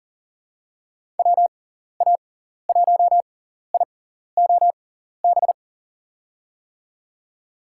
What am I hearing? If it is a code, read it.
Morse code, heard as a keyed tone, about 40 wpm, spaced as for 14 wpm, 700 Hz. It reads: WA1IOB
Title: USA Call Signs - 6 Characters - 6x Character Spacing - 40wpm